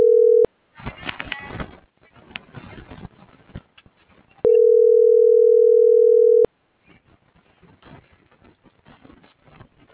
Ringback tone (North America), 440 + 480 Hz.